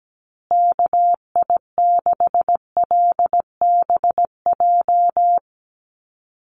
Morse code, heard as a keyed tone, about 17 wpm, 700 Hz. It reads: KI6LBJ
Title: USA Call Signs Repeated 2x 17wpm